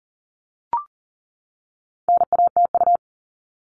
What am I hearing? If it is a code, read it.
Morse code, heard as a keyed tone, about 38 wpm, 700 Hz. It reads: NATU
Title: All Cut Numbers - 4-Digit 40wpm